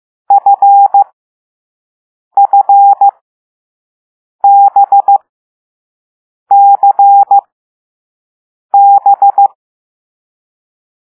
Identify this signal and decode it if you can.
Morse code, heard as a keyed tone, about 15 wpm, spaced as for 5 wpm, 800 Hz. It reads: FFBCB
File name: Morse / Level6B